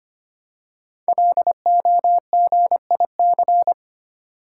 Morse code, heard as a keyed tone, about 25 wpm, 700 Hz. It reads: LOGIC
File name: 5 Letter Words - First 3 Letters Given - 25wpm